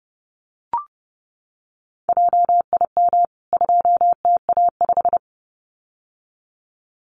Morse code, heard as a keyed tone, about 30 wpm, 700 Hz. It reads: JIM 2TA5
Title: CWT Exchange 30wpm